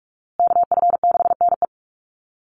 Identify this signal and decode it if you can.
Morse code, heard as a keyed tone, about 44 wpm, 700 Hz. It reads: KF6NE
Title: USA Call Signs Repeated 3x 45wpm